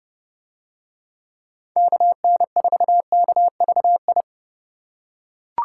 Morse code, heard as a keyed tone, about 30 wpm, 700 Hz. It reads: KN4KVI